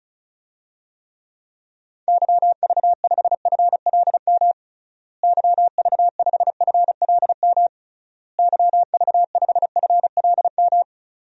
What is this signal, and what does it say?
Morse code, heard as a keyed tone, about 36 wpm, 700 Hz. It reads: YV5FLM YV5FLM YV5FLM
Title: DXCC-100 Call Signs - 6 Characters - Repeated 3x - 35wpm